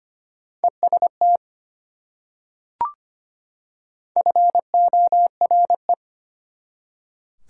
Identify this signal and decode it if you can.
Morse code, heard as a keyed tone, about 25 wpm, 700 Hz. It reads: EST FORE
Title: Prefixes and Suffixes 25wpm